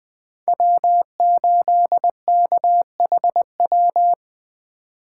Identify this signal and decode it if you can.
Morse code, heard as a keyed tone, about 20 wpm, 700 Hz. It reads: W8KHW